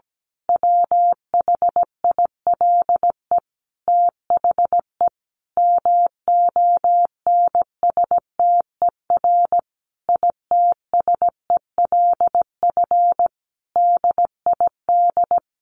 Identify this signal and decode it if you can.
Morse code, heard as a keyed tone, about 17 wpm, 700 Hz. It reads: WHILE THE MONSTER ITSELF DID
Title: A Sound Of Thunder 17wpm